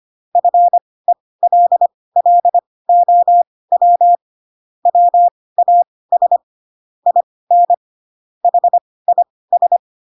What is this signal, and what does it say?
Morse code, heard as a keyed tone, about 25 wpm, spaced as for 18 wpm, 700 Hz. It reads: FELLOW WAS IN HIS